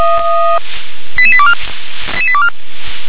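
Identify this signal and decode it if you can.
Five-tone selective call (ZVEI), heard as quick runs of five tones, tones of about 70 ms. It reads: 88824 88824
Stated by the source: ZVEI five-tone selective call received by SDR.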